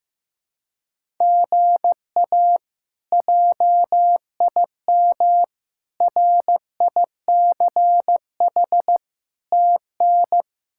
Morse code, heard as a keyed tone, about 15 wpm, 700 Hz. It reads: GA JIM RICH TN
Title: K1USN SST QSO Practice - 15wpm